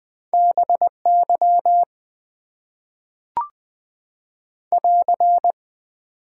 Morse code, heard as a keyed tone, about 20 wpm, 700 Hz. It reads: BY <AR>
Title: Top 90 QSO Elements - 6x Character Spacing - 20wpm